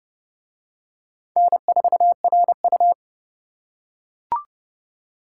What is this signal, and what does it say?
Morse code, heard as a keyed tone, about 29 wpm, 700 Hz. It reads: N4RU